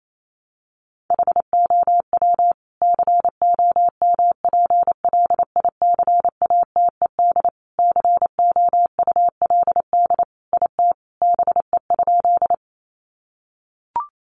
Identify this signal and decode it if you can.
Morse code, heard as a keyed tone, about 28 wpm, 700 Hz. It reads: HOW COMPLICATED COULD IT BE?
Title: Sentences from Top 1200 Words 28wpm